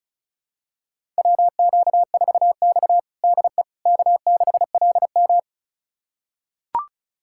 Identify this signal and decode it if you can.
Morse code, heard as a keyed tone, about 36 wpm, 700 Hz. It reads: WQ4X DE K6LM
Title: USA Callsign DE Callsign - 4 Characters - 35wpm